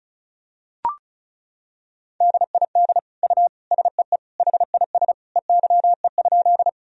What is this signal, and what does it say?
Morse code, heard as a keyed tone, about 34 wpm, 700 Hz. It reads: DID U SEE HIS EYE?
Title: Sentences from Top 300 Words and Abbreviations 35wpm